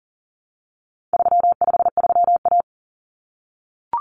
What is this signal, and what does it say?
Morse code, heard as a keyed tone, about 40 wpm, 700 Hz. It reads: 353A
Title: Common Cut Numbers - 4-Digit 40wpm